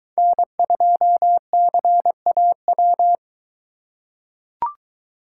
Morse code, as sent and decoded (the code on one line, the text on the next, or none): -. ..--- -.-. .- .--
N2CAW